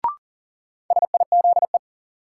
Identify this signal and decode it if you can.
Morse code, heard as a keyed tone, about 37 wpm, 700 Hz. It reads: SIZE
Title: Sets of 3 Words - New Word Drill - 40wpm